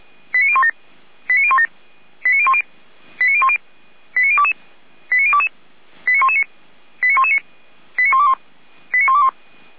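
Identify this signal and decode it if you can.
Five-tone selective call (ZVEI), heard as quick runs of five tones, tones of about 70 ms. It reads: 89017 89017 89010 89010 89022 89022 89119 89119 89121 89121
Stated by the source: ZVEI five-tone selective call received on SDR.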